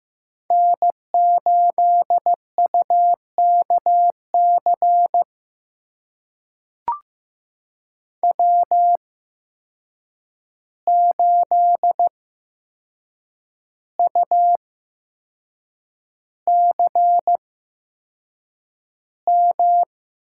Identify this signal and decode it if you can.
Morse code, heard as a keyed tone, about 15 wpm, spaced as for 6 wpm, 700 Hz. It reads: N8UKC W8UCM
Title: USA Call Signs - 5 Characters - 8x Character Spacing - 15wpm